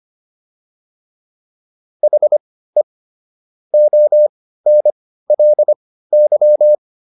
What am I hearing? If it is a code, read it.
Morse code, heard as a keyed tone, about 25 wpm, spaced as for 15 wpm, 600 Hz. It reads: HE ONLY